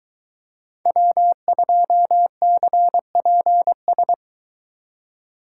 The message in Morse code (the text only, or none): W2CPS